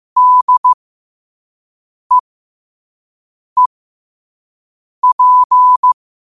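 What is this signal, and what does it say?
Morse code, heard as a keyed tone, about 15 wpm, spaced as for 5 wpm, 1000 Hz. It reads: DEEP
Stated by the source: Morse code sent at 5 wpm.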